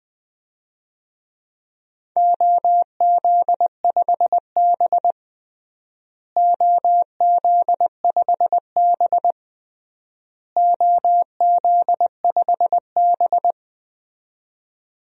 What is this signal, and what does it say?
Morse code, heard as a keyed tone, about 20 wpm, 700 Hz. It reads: OZ5B OZ5B OZ5B
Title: DXCC-100 Call Signs - 4 Characters - Repeated 3x - 20wpm